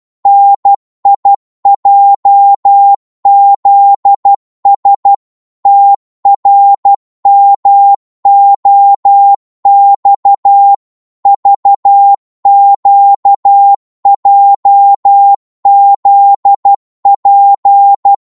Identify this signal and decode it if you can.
Morse code, heard as a keyed tone, about 12 wpm, 800 Hz. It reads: NIJZS TRMOX VQJZP
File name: cwmp3 / ltr-12wpm